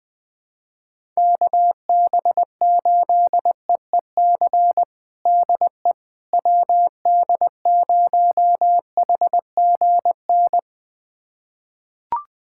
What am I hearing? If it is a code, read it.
Morse code, heard as a keyed tone, about 20 wpm, 700 Hz. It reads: KB8EEC DE WD0HGN
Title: USA Callsign DE Callsign - 6 Characters - 20wpm